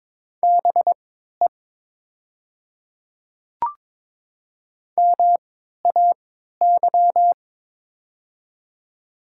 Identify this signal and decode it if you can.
Morse code, heard as a keyed tone, about 22 wpm, spaced as for 13 wpm, 700 Hz. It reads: BE MAY
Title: 3x Character Spacing - Top 100 Words - 22wpm